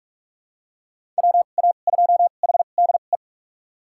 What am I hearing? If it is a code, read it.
Morse code, heard as a keyed tone, about 46 wpm, spaced as for 33 wpm, 700 Hz. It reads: WA2HDE
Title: USA Call Signs - 6 Characters - 2x Character Spacing - 45wpm